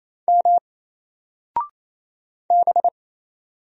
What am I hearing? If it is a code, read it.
Morse code, heard as a keyed tone, about 29 wpm, 700 Hz. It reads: M B
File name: Single-Letter-Number-28wpm